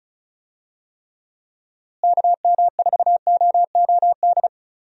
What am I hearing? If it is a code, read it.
Morse code, heard as a keyed tone, about 36 wpm, 700 Hz. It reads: KM4OOD
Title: USA Call Signs - 6 Characters - Repeated 3x - 35wpm